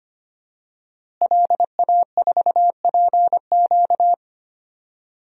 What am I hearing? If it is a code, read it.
Morse code, heard as a keyed tone, about 25 wpm, 700 Hz. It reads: LA4PQ
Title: DXCC-100 Call Signs - 5 Characters - Repeated 2x - 25wpm